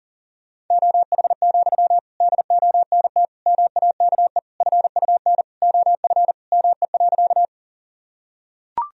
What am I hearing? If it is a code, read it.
Morse code, heard as a keyed tone, about 37 wpm, 700 Hz. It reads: OH, DONT MAKE FUN OF ME.
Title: Sentences from Top 1000 Words - Review Part 4 - 40wpm